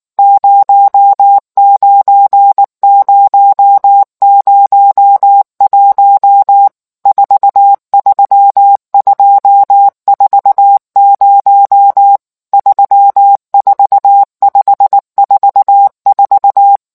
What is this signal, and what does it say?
Morse code, heard as a keyed tone, about 19 wpm, 800 Hz. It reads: 09001 43240 34544